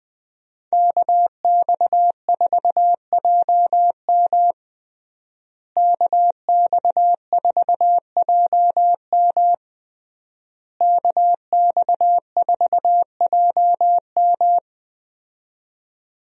Morse code, heard as a keyed tone, about 21 wpm, 700 Hz. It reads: KX4JM KX4JM KX4JM
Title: USA Call Signs - 5 Characters - Repeated 3x - 20wpm